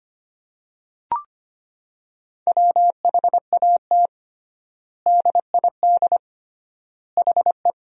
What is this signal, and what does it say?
Morse code, heard as a keyed tone, about 25 wpm, 700 Hz. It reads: WHAT DID HE
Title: 3x Word-Spacing - Sentences from Top 200 Words - 25wpm